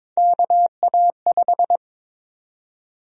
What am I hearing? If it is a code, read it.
Morse code, heard as a keyed tone, about 21 wpm, 700 Hz. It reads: KA5
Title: USA Call Sign Prefixes - 22wpm